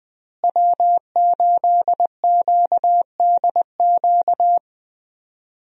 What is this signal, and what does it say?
Morse code, heard as a keyed tone, about 20 wpm, 700 Hz. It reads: W8QDQ
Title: USA Call Signs - 5 Characters - Repeated 2x - 20wpm